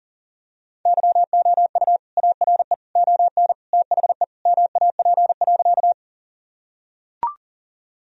Morse code, heard as a keyed tone, about 40 wpm, 700 Hz. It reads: YOU ARE ON THE MAP.
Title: Sentences from Top 1000 Words - Review Part 2 - 40wpm